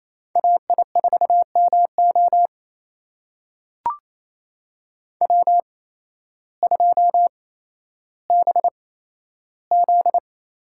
Morse code, heard as a keyed tone, about 28 wpm, spaced as for 13 wpm, 700 Hz. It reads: AI4MO W2BZ
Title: USA Call Signs - 8x Character Spacing - 28wpm